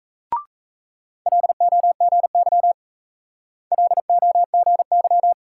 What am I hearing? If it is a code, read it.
Morse code, heard as a keyed tone, about 36 wpm, 700 Hz. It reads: LOGY LOGY